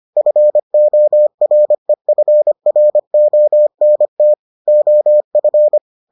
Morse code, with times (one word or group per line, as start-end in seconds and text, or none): 0.2-4.3 FOREFRONT
4.7-5.8 OF